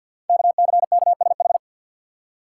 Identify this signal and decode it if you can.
Morse code, heard as a keyed tone, about 51 wpm, 700 Hz. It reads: KCKSH
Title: Code Groups - 5 Letters - 50wpm